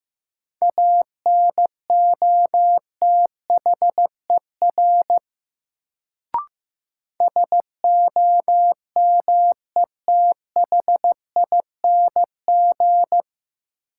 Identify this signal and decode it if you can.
Morse code, heard as a keyed tone, about 15 wpm, 700 Hz. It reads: ANOTHER SOMETHING
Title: Sentences from Top 600 Words - New Word Drill - 15wpm